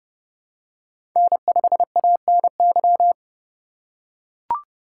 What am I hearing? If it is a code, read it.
Morse code, heard as a keyed tone, about 31 wpm, 700 Hz. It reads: N5ANY
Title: USA Call Signs - 5 Characters - 2x Character Spacing - 30wpm